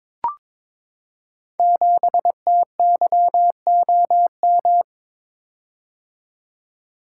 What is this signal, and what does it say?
Morse code, heard as a keyed tone, about 22 wpm, 700 Hz. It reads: 7TYOM
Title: Code Groups - 5 Letter:Numbers - 22wpm